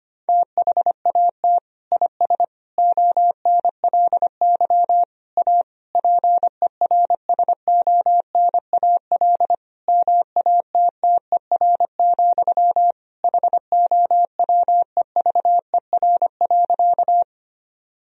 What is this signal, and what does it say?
Morse code, heard as a keyed tone, about 25 wpm, 700 Hz. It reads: THAT IS ONLY A PERSONAL MATTER, HOWEVER.
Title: Sentences from Top 1000 Words - Review Part 4 - 25wpm